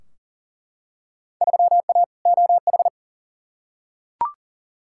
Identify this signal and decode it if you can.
Morse code, heard as a keyed tone, about 40 wpm, 700 Hz. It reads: 3A OH
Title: Field Day Exchanges - 40wpm